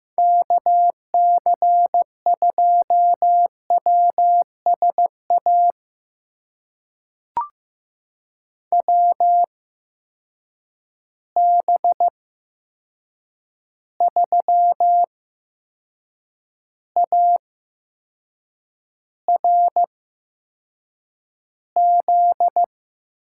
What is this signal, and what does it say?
Morse code, heard as a keyed tone, about 15 wpm, spaced as for 6 wpm, 700 Hz. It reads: KC2WSA WB3ARZ
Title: USA Call Signs - 6 Characters - 8x Character Spacing - 15wpm